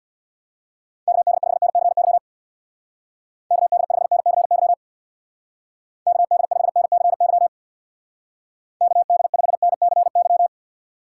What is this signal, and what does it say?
Morse code, heard as a keyed tone, about 62 wpm, 700 Hz. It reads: KD5NCY KD5NCY KD5NCY KD5NCY
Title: Speed Racing - USA Call Signs 50wpm